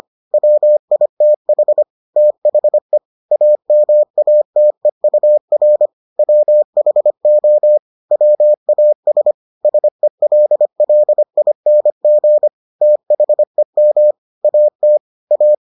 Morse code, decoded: WITH THE AMATEUR WHO WAS SELLING THEM AT A

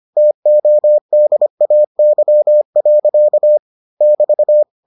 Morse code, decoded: TODAY. <BT>

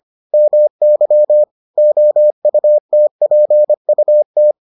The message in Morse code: MY OUTPUT

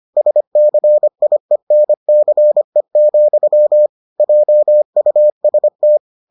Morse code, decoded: SCIENCE, JUST

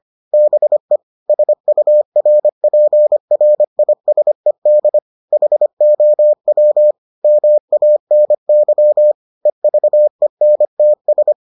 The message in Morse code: BE SURPRISED HOW MANY EVENTS

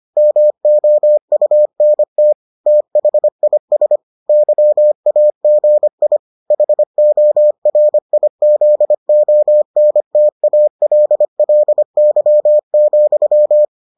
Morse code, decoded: MOUNT THIS YAGI HORIZONTALLY,

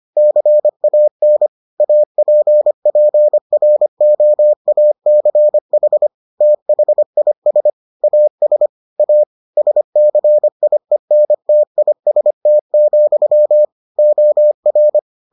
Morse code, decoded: CAN APPROACH THIS AS A SCIENTIST, OR

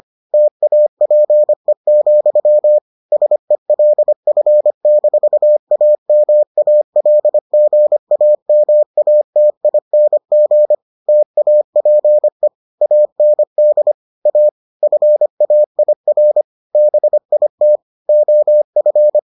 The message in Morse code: TAPE, SELF-AMALGAMATING TAPE AND A FAIR BIT OF

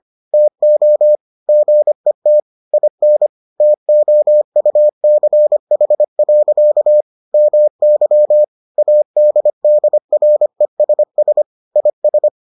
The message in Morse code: TO GET IN TOUCH. MY ADDRESS IS

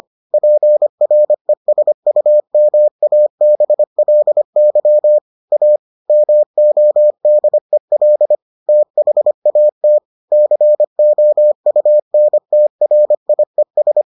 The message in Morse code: PRESUMABLY A MODEL THAT COUNTRIES

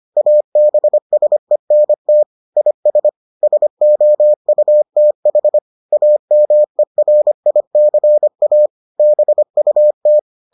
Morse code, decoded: ABSENT IS SOUTH AMERICA BUT

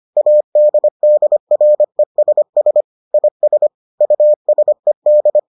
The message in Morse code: ADDRESS IS USED

